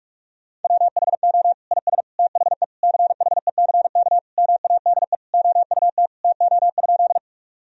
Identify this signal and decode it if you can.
Morse code, heard as a keyed tone, about 46 wpm, 700 Hz. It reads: WHO IS THE CHECK MADE OUT TO?
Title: Sentences from Top 1000 Words - Review Part 3 - 45wpm